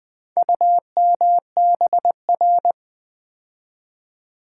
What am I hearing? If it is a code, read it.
Morse code, heard as a keyed tone, about 20 wpm, 700 Hz. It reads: UMBR
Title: Build A Word - 4 Letters At A Time - 8-Letter Words - 20wpm